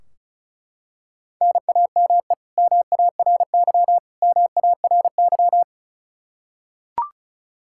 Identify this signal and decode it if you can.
Morse code, heard as a keyed tone, about 34 wpm, 700 Hz. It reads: NAME MARY MARY
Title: K1USN SST QSO Practice 35wpm